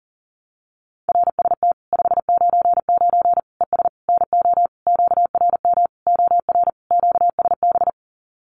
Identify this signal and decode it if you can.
Morse code, heard as a keyed tone, about 40 wpm, 700 Hz. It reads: RST 599 ES NO QRM OR QSB
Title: Common QSO Phrases 40wpm